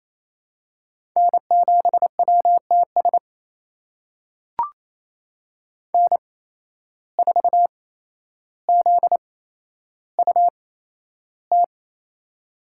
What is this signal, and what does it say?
Morse code, heard as a keyed tone, about 27 wpm, spaced as for 12 wpm, 700 Hz. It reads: N7WTS N4ZUT
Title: USA Call Signs - 5 Characters - 8x Character Spacing - 28wpm